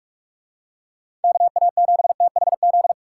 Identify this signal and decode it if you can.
Morse code, heard as a keyed tone, about 44 wpm, 700 Hz. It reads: KA7THZ